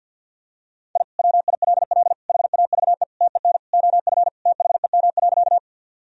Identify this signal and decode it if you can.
Morse code, heard as a keyed tone, about 47 wpm, 700 Hz. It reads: I WILL HAVE TEN OF THEM.
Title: Sentences from Top 1000 Words - Review Part 2 - 50wpm